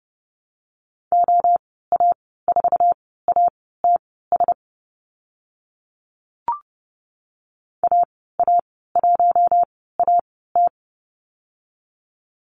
Morse code, heard as a keyed tone, about 30 wpm, spaced as for 17 wpm, 700 Hz. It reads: OA4ATS AA1AT
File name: Lesson 15g - Number 5 - Call Signs 30wpm